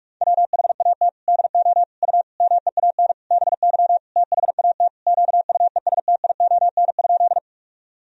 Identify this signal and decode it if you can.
Morse code, heard as a keyed tone, about 41 wpm, 700 Hz. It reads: WHAT DO U MEAN BY THAT QUESTION?
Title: Sentences from Top 400 Words and Abbreviations 45wpm